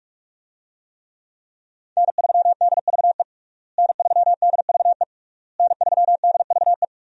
Morse code, heard as a keyed tone, about 46 wpm, 700 Hz. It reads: N3DVE N3DVE N3DVE
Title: USA Call Signs - 5 Characters - Repeated 3x - 45wpm